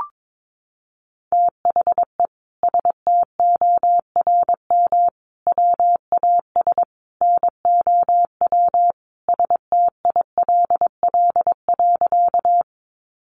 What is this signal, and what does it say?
Morse code, heard as a keyed tone, about 22 wpm, 700 Hz. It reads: THE STORM WAS NOW STILL.
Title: Sentences from Top 1500 Words 22wpm